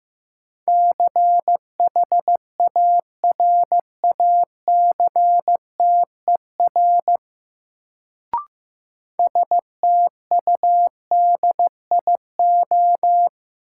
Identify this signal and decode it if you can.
Morse code, heard as a keyed tone, about 15 wpm, 700 Hz. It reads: CHARACTER STUDIO
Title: Sentences from Top 1200 Words - New Word Drill Alt - 15wpm